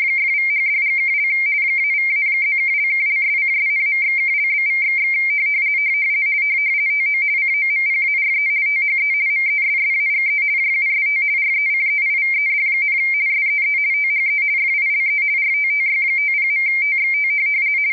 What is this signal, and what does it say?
RTTY, about 50 baud. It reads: !=#$&() -=-+()()':',./()?=// TO COMPARE THE MESSAGE JUST RECEIVED, SEE THE FILE RTTY50.TXT
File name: rtty170 / rtty50.wav / rtty50